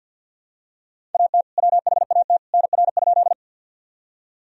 Morse code, heard as a keyed tone, about 50 wpm, 700 Hz. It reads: AT WHAT NR?
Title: Sentences from Top 100 Words and Abbreviations 50wpm